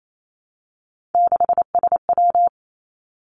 Morse code, heard as a keyed tone, about 28 wpm, 700 Hz. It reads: 6SW
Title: Three Challenging Letter-Number Combinations 28wpm